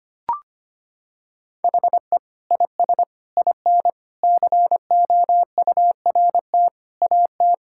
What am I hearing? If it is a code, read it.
Morse code, heard as a keyed tone, about 25 wpm, 700 Hz. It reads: HE IS IN COURT AT
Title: Sentences from Top 1600 Words 25wpm